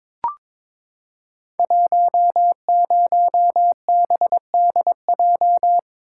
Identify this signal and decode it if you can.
Morse code, heard as a keyed tone, about 22 wpm, 700 Hz. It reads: 10BDJ
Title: Code Groups - 5 Letter:Numbers - 22wpm